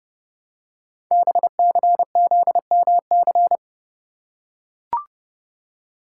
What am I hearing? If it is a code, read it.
Morse code, heard as a keyed tone, about 30 wpm, 700 Hz. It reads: BCZMC
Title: Code Groups - 5 Letters - 30wpm